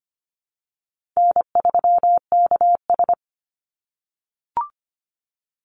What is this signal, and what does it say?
Morse code, heard as a keyed tone, about 25 wpm, 700 Hz. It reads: N3KS
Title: Easy Call Signs - 25wpm